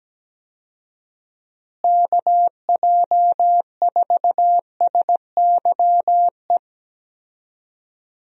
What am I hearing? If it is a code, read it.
Morse code, heard as a keyed tone, about 17 wpm, 700 Hz. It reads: KJ4SYE
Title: USA Call Signs - 6 Characters - Repeated 2x - 17wpm